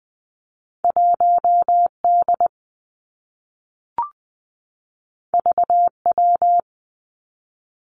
Morse code, heard as a keyed tone, about 20 wpm, 700 Hz. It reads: 1 D V W
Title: Two Challenging Letter-Number Combinations 20wpm